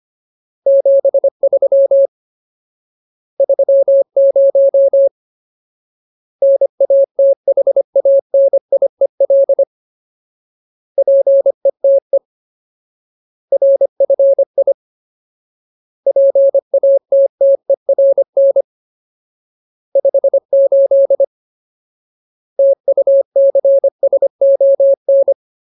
Morse code, heard as a keyed tone, about 25 wpm, 550 Hz. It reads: 73 30 NATHANIEL PETE RFI PATTERN 58 TUCSON
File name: Ultimate Word List / 25WPM Ultimate